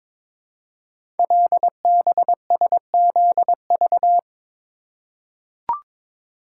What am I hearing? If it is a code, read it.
Morse code, heard as a keyed tone, about 22 wpm, 700 Hz. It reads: LBSZV
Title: Code Groups - 5 Letters - 22wpm